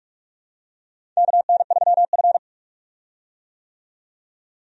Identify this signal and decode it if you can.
Morse code, heard as a keyed tone, about 46 wpm, 700 Hz. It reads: KN3F
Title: USA Call Signs - 4 Characters - 45wpm